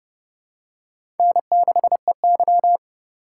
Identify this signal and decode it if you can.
Morse code, heard as a keyed tone, about 30 wpm, 700 Hz. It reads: N6EY
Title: USA Call Signs - 4 Characters - 4x Character Spacing - 30wpm